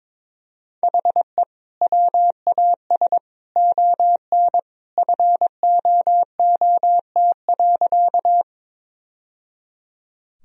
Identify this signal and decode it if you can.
Morse code, heard as a keyed tone, about 22 wpm, 700 Hz. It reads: HE WAS ON FOOT.
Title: Sentences from Top 1000 Words - Review Part 3 - 22wpm